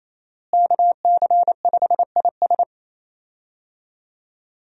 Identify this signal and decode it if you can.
Morse code, heard as a keyed tone, about 27 wpm, 700 Hz. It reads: KC5IS
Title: USA Call Signs - 5 Characters - Repeated 3x - 28wpm